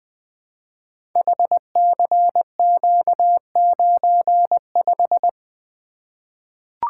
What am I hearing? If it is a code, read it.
Morse code, heard as a keyed tone, about 20 wpm, 700 Hz. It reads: HCQ95
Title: Code Groups - 5 Letter:Numbers - 20wpm